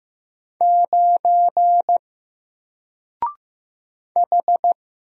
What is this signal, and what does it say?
Morse code, heard as a keyed tone, about 15 wpm, 700 Hz. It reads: 9 H
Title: Single Letter-Number Alt 15wpm